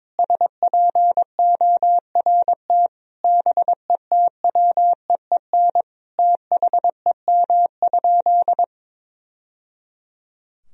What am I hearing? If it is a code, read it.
Morse code, heard as a keyed tone, about 22 wpm, 700 Hz. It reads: SPORT BETWEEN THEM?
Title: Sentences from Top 1000 Words - Review Part 4 - 22wpm